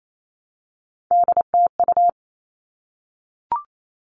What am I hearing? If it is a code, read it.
Morse code, heard as a keyed tone, about 28 wpm, 700 Hz. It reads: DTU
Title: All Cut Numbers - 3-Digit 28wpm